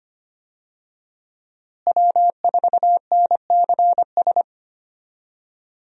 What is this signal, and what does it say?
Morse code, heard as a keyed tone, about 25 wpm, 700 Hz. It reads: W4NCS
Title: USA Call Signs - 5 Characters - Repeated 3x - 25wpm